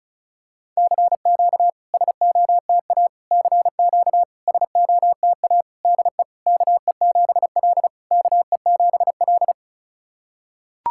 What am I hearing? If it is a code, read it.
Morse code, heard as a keyed tone, about 36 wpm, 690 Hz. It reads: CQ SOTA CQ SOTA DE KE7L KE7L
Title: SOTA QSO Practice 35wpm